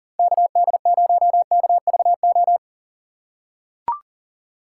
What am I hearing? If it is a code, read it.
Morse code, heard as a keyed tone, about 40 wpm, 700 Hz. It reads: KD0KVO
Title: USA Call Signs - 6 Characters - 6x Character Spacing - 40wpm